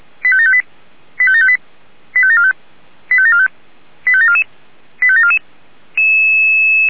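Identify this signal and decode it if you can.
Five-tone selective call (ZVEI), heard as quick runs of five tones, tones of about 70 ms. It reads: 85758 85758 85745 85745 85744 85744
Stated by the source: ZVEI five-tone call 85735 received by SDR.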